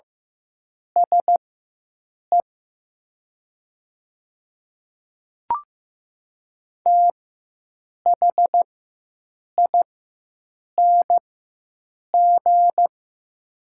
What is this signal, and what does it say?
Morse code, heard as a keyed tone, about 15 wpm, spaced as for 7 wpm, 700 Hz. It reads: SE THING
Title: 4x Character Spacing - Top 100 Words - 15wpm